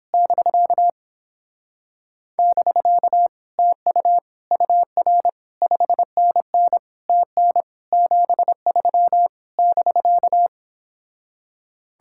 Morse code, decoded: <BK> <BK> TU UR 5NN TN 73 <BK>

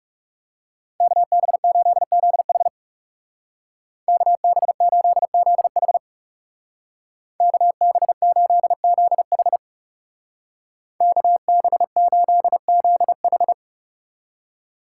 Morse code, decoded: KB8ZH KB8ZH KB8ZH KB8ZH

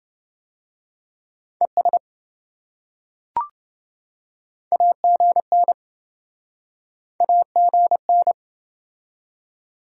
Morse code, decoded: ES AGN AGN

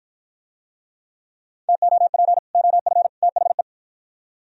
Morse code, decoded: TOP OF THE

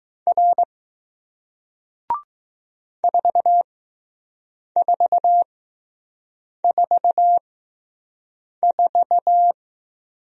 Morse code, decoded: R 4444

